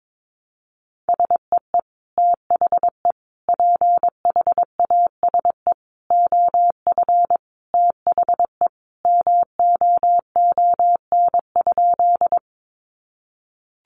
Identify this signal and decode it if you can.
Morse code, heard as a keyed tone, about 22 wpm, 700 Hz. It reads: SEE THE PHASE OF THE MOON?
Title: Sentences from Top 1200 Words 22wpm